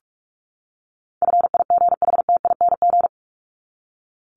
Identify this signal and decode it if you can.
Morse code, heard as a keyed tone, about 44 wpm, 700 Hz. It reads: FIGHTING
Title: ING Suffix - Top 100 Words with Base Words 45wpm